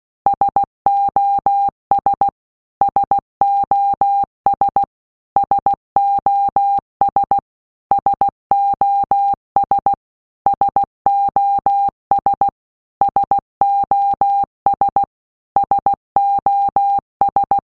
Morse code, heard as a keyed tone, about 16 wpm, 800 Hz. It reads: SOS SOS SOS SOS SOS SOS SOS
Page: SOS sound in Morse Code (Slow)